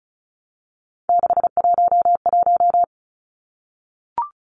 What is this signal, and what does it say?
Morse code, heard as a keyed tone, about 34 wpm, 700 Hz. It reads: 611
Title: Numbers - 3-Digit 35wpm